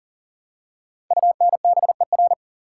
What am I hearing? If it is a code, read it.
Morse code, heard as a keyed tone, about 42 wpm, 700 Hz. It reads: UNBER
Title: Code Groups - 5 Letters - 40wpm